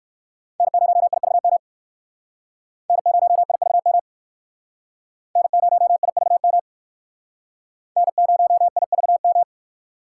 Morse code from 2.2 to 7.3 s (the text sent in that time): N0IVM N0IVM